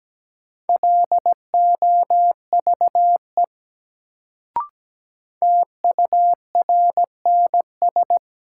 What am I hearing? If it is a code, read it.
Morse code, heard as a keyed tone, about 17 wpm, 700 Hz. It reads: LOVE TURNS
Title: Sets of 5 Words Encore - New Word Drill - 17wpm